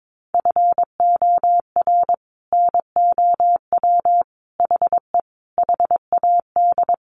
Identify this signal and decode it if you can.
Morse code, heard as a keyed tone, about 22 wpm, 700 Hz. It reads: FOR NOW HE HAD
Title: Sentences from Top 1100 Words 22wpm